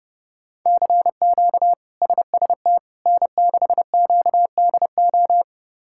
Morse code, heard as a keyed tone, about 30 wpm, 700 Hz. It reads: CQ SST N6QDO
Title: K1USN SST QSO Practice - 30wpm